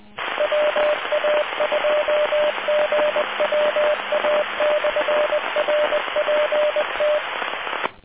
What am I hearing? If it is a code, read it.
Morse code, heard as a keyed tone, about 20 wpm, 600 Hz, with noise. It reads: WA2GWA/RPT